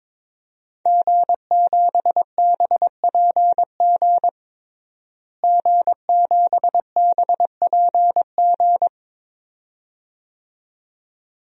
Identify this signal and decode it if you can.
Morse code, heard as a keyed tone, about 21 wpm, 700 Hz. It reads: G7BPG G7BPG
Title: DXCC-100 Call Signs - 5 Characters - Repeated 2x - 22wpm